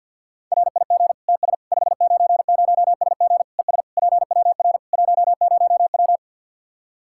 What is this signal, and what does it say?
Morse code, heard as a keyed tone, about 52 wpm, 700 Hz. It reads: RIG TS 590SG ES PWR 10W
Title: Practice QSO - Exchange 2 - Info - 50wpm